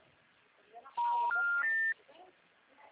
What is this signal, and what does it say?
Special information tone, with speech.